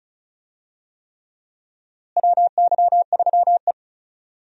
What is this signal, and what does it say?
Morse code, heard as a keyed tone, about 36 wpm, 700 Hz. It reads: WY3E
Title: USA Call Signs - 4 Characters - Repeated 3x - 35wpm